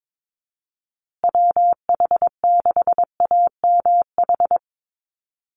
Morse code, heard as a keyed tone, about 22 wpm, 700 Hz. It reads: WH6AMH
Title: USA Call Signs Repeated 3x 22wpm